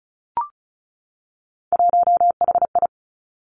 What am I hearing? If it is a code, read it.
Morse code, heard as a keyed tone, about 36 wpm, 700 Hz. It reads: 1HI
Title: Three Challenging Letter-Number Combinations 35wpm